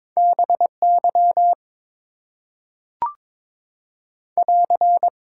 Morse code, heard as a keyed tone, about 23 wpm, 700 Hz. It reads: BY <AR>
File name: Top 90 QSO Elements - 6x Character Spacing - 22wpm